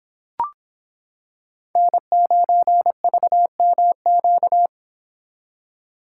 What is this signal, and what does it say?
Morse code, heard as a keyed tone, about 26 wpm, 700 Hz. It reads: N9VMQ